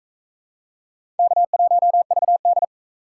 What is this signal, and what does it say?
Morse code, heard as a keyed tone, about 38 wpm, 700 Hz. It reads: K1VD